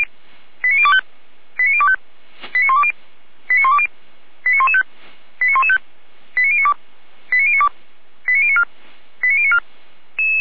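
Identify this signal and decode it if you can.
Five-tone selective call (ZVEI), heard as quick runs of five tones, tones of about 70 ms. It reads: 89025 89025 89120 89120 89115 89115 89092 89092 89094 89094
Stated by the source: ZVEI five-tone selective call received by SDR.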